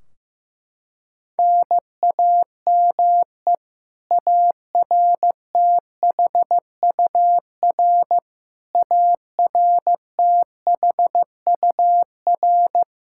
Morse code, heard as a keyed tone, about 15 wpm, 700 Hz. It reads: NAME ARTHUR ARTHUR
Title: Practice QSO - Exchange 1 - Beginning - 15wpm